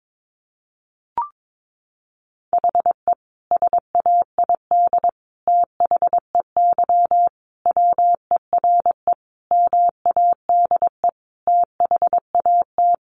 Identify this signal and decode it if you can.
Morse code, heard as a keyed tone, about 22 wpm, 700 Hz. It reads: HE SAID THEY WERE MADE THAT
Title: Sentences from Top 700 Words 22wpm